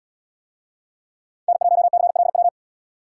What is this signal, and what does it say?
Morse code, heard as a keyed tone, about 77 wpm, 700 Hz. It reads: N1CLW